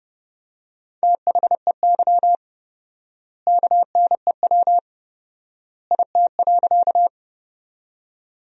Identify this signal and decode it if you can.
Morse code, heard as a keyed tone, about 30 wpm, 700 Hz. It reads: THEY KNEW IT.
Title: 4x Word-Spacing - Sentences from Top 400 Words - 30wpm